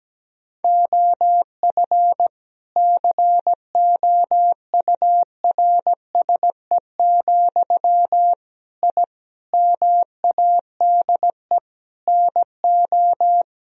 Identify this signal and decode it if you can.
Morse code, heard as a keyed tone, about 17 wpm, 700 Hz. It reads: OF COURSE, I MADE NO
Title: Sentences from Top 1000 Words - Review Part 1 - 17wpm